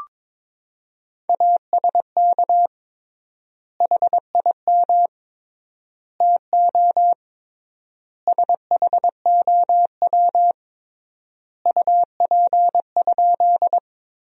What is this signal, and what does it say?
Morse code, heard as a keyed tone, about 22 wpm, 700 Hz. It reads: ASK HIM TO SHOW UP?
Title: 3x Word-Spacing - Sentences from Top 200 Words - 22wpm